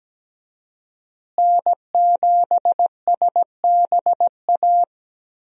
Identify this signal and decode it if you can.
Morse code, heard as a keyed tone, about 17 wpm, 700 Hz. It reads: N7SBA
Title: USA Call Signs - 5 Characters - Repeated 3x - 17wpm